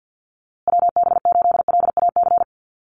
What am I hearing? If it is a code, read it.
Morse code, heard as a keyed tone, about 48 wpm, 700 Hz. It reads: WB8LAC
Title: USA Call Signs Repeated 2x 50wpm